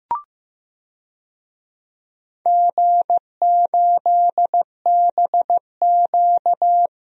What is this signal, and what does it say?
Morse code, heard as a keyed tone, about 15 wpm, 700 Hz. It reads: G8BQ
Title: DXCC-100 Call Signs - 4 Characters - Repeated 2x - 15wpm